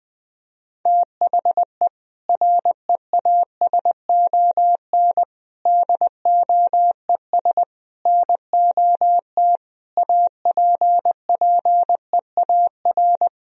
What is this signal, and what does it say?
Morse code, heard as a keyed tone, about 20 wpm, 700 Hz. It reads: THE REASON DOES NOT APPEAR
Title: Sentences from Top 1000 Words - Review Part 3 - 20wpm